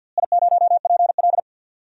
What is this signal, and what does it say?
Morse code, heard as a keyed tone, about 47 wpm, 700 Hz. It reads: I0PL
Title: DXCC-100 Call Signs - 4 Characters - Repeated 3x - 50wpm